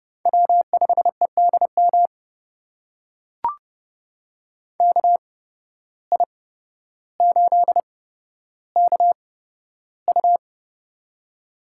Morse code, heard as a keyed tone, about 30 wpm, spaced as for 13 wpm, 700 Hz. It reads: W5EDM KI8KU
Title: USA Call Signs - 5 Characters - 8x Character Spacing - 30wpm